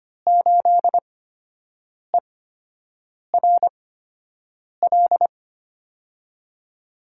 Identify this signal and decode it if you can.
Morse code, heard as a keyed tone, about 25 wpm, spaced as for 7 wpm, 700 Hz. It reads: 8ERL
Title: USA Call Signs - 5 Characters - 8x Character Spacing - 25wpm